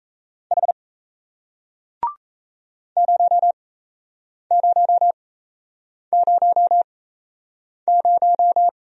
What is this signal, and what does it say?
Morse code, heard as a keyed tone, about 38 wpm, spaced as for 8 wpm, 700 Hz. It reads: H 0000